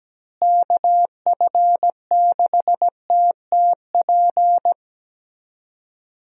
Morse code, heard as a keyed tone, about 17 wpm, 700 Hz. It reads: KF6TTP
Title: USA Call Signs - 6 Characters - Repeated 3x - 17wpm